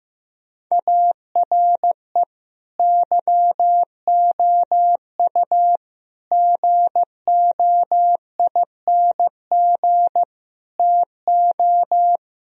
Morse code, heard as a keyed tone, about 15 wpm, 700 Hz. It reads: ARE YOU GOING TO
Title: Sentences from Top 1000 Words - Review Part 1 - 15wpm